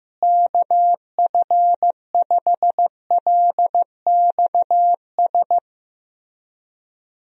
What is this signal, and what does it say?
Morse code, heard as a keyed tone, about 15 wpm, 700 Hz. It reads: KF5LXS
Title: USA Call Signs - 6 Characters - Repeated 2x - 15wpm